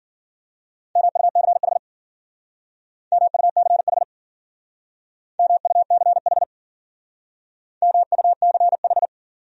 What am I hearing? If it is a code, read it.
Morse code, heard as a keyed tone, about 50 wpm, 700 Hz. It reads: MUCH MUCH MUCH MUCH
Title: Speed-Racing Top 200 Words 40wpm